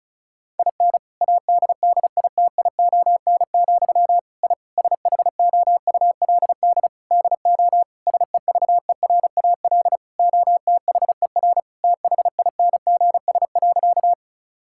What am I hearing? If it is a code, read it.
Morse code, heard as a keyed tone, about 36 wpm, 700 Hz. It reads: IN ADDITION, I SHOULD DO SEVERAL OTHER THINGS.
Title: Sentences from Top 1000 Words - Review Part 6 - 35wpm